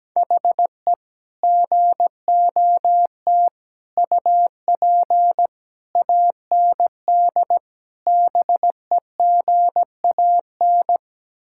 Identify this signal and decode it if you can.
Morse code, heard as a keyed tone, about 17 wpm, 700 Hz. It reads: HE GOT UP AND BEGAN
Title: Sentences from Top 1600 Words 17wpm